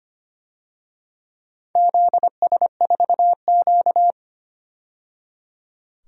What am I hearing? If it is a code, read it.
Morse code, heard as a keyed tone, about 25 wpm, 700 Hz. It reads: ZS4Q
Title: DXCC-100 Call Signs - 4 Characters - 25wpm